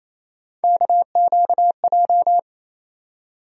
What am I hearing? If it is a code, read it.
Morse code, heard as a keyed tone, about 28 wpm, 700 Hz. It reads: KQJ
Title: USA Call Sign Postfixes - 3 Letters 28wpm